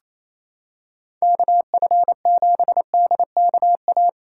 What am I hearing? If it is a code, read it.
Morse code, heard as a keyed tone, about 28 wpm, 700 Hz. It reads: KF7DKA
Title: USA Callsign DE Callsign - 6 Characters - 28wpm